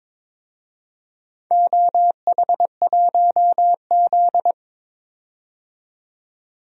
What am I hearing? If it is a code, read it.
Morse code, heard as a keyed tone, about 22 wpm, 700 Hz. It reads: OH1Z
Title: DXCC-100 Call Signs - 4 Characters - 22wpm